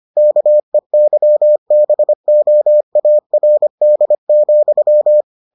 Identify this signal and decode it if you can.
Morse code, heard as a keyed tone, about 25 wpm, 600 Hz. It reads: KEYBOARD,